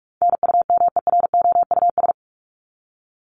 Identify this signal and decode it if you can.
Morse code, heard as a keyed tone, about 44 wpm, 700 Hz. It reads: NUMEROUS
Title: Top 1251-1500 Words 45wpm